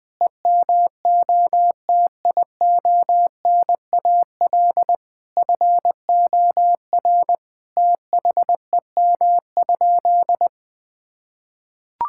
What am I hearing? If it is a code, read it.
Morse code, heard as a keyed tone, about 20 wpm, 700 Hz. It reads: EMOTIONAL FOR THEM?
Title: Sentences from Top 1000 Words - Review Part 5 - 20wpm